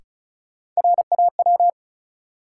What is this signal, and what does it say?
Morse code, heard as a keyed tone, about 36 wpm, 700 Hz. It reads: RAW
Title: Sentences from Top 1500 Words - New Word Drill - 35wpm